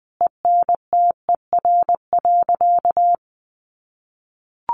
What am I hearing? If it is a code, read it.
Morse code, heard as a keyed tone, about 20 wpm, 700 Hz. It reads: ENTER.
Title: Sentences from Top 1300 Words 20wpm